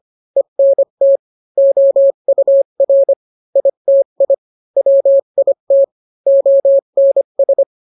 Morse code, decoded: ENT OUR ITI WIT ONS